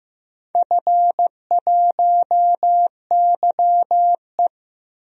Morse code, heard as a keyed tone, about 15 wpm, 700 Hz. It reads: F1YE
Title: DXCC-100 Call Signs - 4 Characters - Repeated 3x - 15wpm